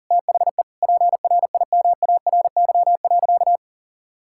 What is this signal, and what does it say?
Morse code, heard as a keyed tone, about 40 wpm, 700 Hz. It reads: THE PRIMARY.
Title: Sentences from Top 1000 Words - Review Part 5 - 40wpm